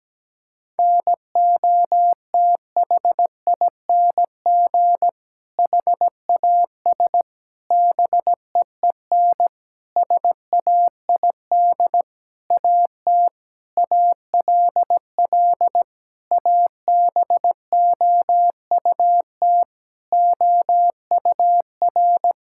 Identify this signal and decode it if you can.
Morse code, heard as a keyed tone, about 17 wpm, 700 Hz. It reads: NOTHING HAS BEEN SAID AT ALL ABOUT OUR
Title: Sentences from Top 1000 Words - Review Part 5 - 17wpm